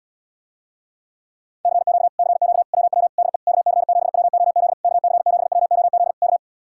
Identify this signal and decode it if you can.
Morse code, heard as a keyed tone, about 80 wpm, 700 Hz. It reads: CQ CQ WW DE KC7WOZ KC7WOZ K